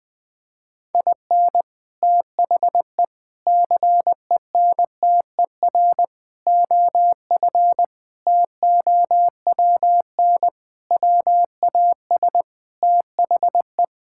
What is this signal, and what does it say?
Morse code, heard as a keyed tone, about 20 wpm, 700 Hz. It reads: IN THE CENTER OF TOWN WAS THE
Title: Sentences with 4-Word Phrases - 20wpm